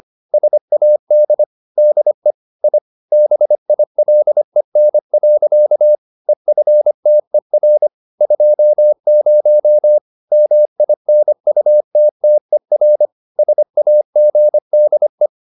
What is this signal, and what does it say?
Morse code, heard as a keyed tone, about 25 wpm, 600 Hz. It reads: SAD DE I BILEN. EFTER 20 MINUTTER SAGDE
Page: Ferie 25 wpm